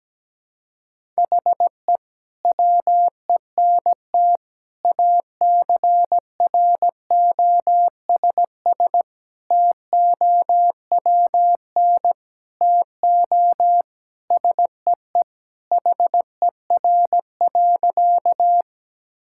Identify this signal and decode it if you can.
Morse code, heard as a keyed tone, about 17 wpm, 700 Hz. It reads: HE WENT ACROSS TOWN TO SEE HER.
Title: Sentences from Top 1000 Words - Review Part 5 - 17wpm